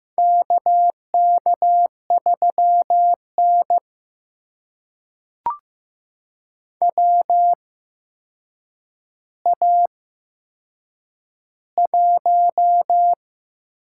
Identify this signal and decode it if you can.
Morse code, heard as a keyed tone, about 15 wpm, spaced as for 7 wpm, 700 Hz. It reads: KK3N WA1
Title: USA Call Signs - 4 Characters - 8x Character Spacing - 15wpm